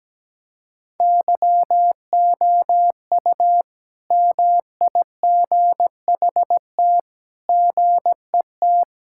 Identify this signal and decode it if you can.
Morse code, heard as a keyed tone, about 17 wpm, 700 Hz. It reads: YOU MIGHT GET
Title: Sentences from Top 1000 Words - Review Part 5 - 17wpm